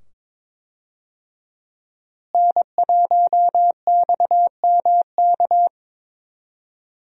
Morse code, heard as a keyed tone, about 22 wpm, 700 Hz. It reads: N1XMK
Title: USA Call Signs - 5 Characters - Repeated 2x - 22wpm